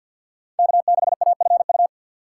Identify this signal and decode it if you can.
Morse code, heard as a keyed tone, about 53 wpm, 700 Hz. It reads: K6AFU
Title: USA Call Signs - 5 Characters - 6x Character Spacing - 50wpm